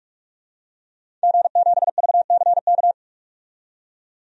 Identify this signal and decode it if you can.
Morse code, heard as a keyed tone, about 46 wpm, 700 Hz. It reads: G7VCK
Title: DXCC-100 Call Signs - 5 Characters - 45wpm